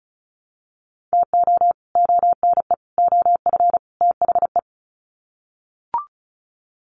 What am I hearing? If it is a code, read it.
Morse code, heard as a keyed tone, about 34 wpm, 700 Hz. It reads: TO ONE OF THE
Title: Sets of 4 Words 35wpm